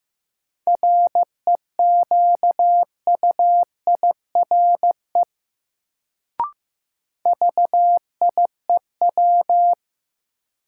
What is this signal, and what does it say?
Morse code, heard as a keyed tone, about 15 wpm, 700 Hz. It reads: REQUIRE VIEW
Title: Sentences from Top 900 Words - New Word Drill - 15wpm